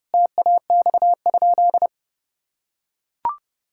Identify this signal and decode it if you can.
Morse code, heard as a keyed tone, about 30 wpm, 700 Hz. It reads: TAX?
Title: Sentences from Top 1000 Words - Review Part 5 - 30wpm